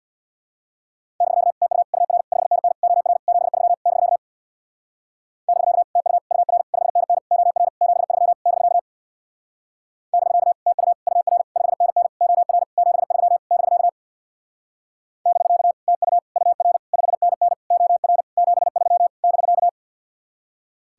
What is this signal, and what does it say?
Morse code, heard as a keyed tone, about 66 wpm, 700 Hz. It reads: <BK> TU UR 5NN OR 73 <BK> <BK> TU UR 5NN OR 73 <BK> <BK> TU UR 5NN OR 73 <BK> <BK> TU UR 5NN OR 73 <BK>